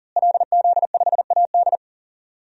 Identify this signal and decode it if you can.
Morse code, heard as a keyed tone, about 38 wpm, 700 Hz. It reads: LZ5AD
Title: DXCC-100 Call Signs - 5 Characters - Repeated 2x - 40wpm